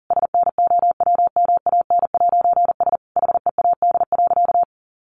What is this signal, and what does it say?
Morse code, heard as a keyed tone, about 37 wpm, 700 Hz. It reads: SNOWMAN'S HEAD.
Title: Easy Phrases 40wpm